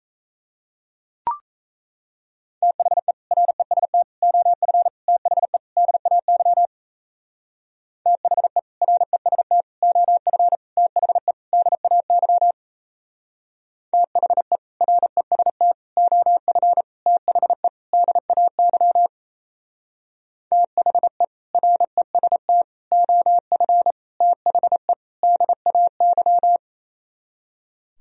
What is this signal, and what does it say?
Morse code, heard as a keyed tone, about 34 wpm, 700 Hz. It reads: THE REST OF THE DAY THE REST OF THE DAY THE REST OF THE DAY THE REST OF THE DAY
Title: Speed-Racing - Sets of 5 Words 28wpm